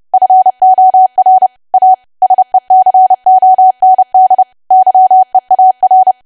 Morse code, heard as a keyed tone, about 30 wpm, 750 Hz. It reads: FOR A SECOND YEAR
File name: cwa / W1AW-Level3 / 30WPM